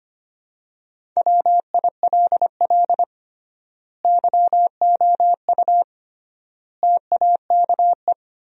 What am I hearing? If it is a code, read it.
Morse code, heard as a keyed tone, about 25 wpm, 700 Hz. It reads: WILL YOU TAKE
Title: 3x Word-Spacing - Sentences from Top 200 Words - 25wpm